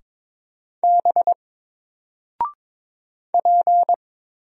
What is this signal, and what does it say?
Morse code, heard as a keyed tone, about 22 wpm, 700 Hz. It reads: B P
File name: Single-Letter-Number-22wpm